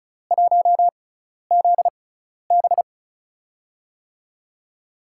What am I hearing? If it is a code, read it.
Morse code, heard as a keyed tone, about 34 wpm, spaced as for 12 wpm, 700 Hz. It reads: 1ZB
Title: USA Call Signs - 6x Character Spacing - 35wpm